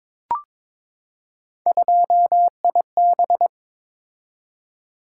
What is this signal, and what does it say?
Morse code, heard as a keyed tone, about 22 wpm, 700 Hz. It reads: 2IB
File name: practice / 22wpm / Three Letter-Number Combinations - 22wpm